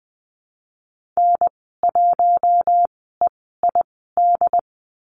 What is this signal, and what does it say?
Morse code, heard as a keyed tone, about 20 wpm, spaced as for 14 wpm, 700 Hz. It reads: N1EID
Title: 088 Lesson 15g - Number 5 - Call Signs 20wpm